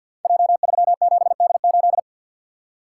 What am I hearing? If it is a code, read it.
Morse code, heard as a keyed tone, about 50 wpm, 700 Hz. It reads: J37D8
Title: Code Groups - 5 Letter:Numbers - 50wpm